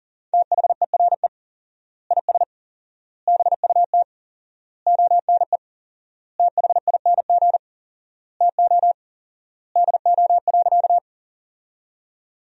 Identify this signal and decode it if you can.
Morse code, heard as a keyed tone, about 40 wpm, 700 Hz. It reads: THERE IS BUT ONE THING TO DO.
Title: 4x Word-Spacing - Sentences from Top 100 Words - 40wpm